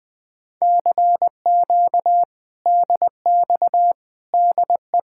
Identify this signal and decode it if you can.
Morse code, heard as a keyed tone, about 21 wpm, 700 Hz. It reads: CQ DX DE
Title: Speed-Racing Common QSO Phrases - 20wpm